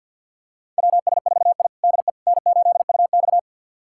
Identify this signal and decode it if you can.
Morse code, heard as a keyed tone, about 52 wpm, 700 Hz. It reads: WS4I DE N8UX
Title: USA Callsign DE Callsign - 4 Characters - 50wpm